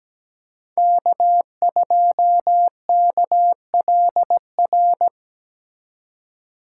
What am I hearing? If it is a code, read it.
Morse code, heard as a keyed tone, about 17 wpm, 700 Hz. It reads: K2KLR
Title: USA Call Signs - 5 Characters - 8x Character Spacing - 17wpm